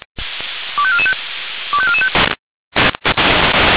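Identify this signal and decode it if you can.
Five-tone selective call (ZVEI), heard as quick runs of five tones, tones of about 70 ms. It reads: 25445 25445
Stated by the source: ZVEI five-tone selective call received on an SDR.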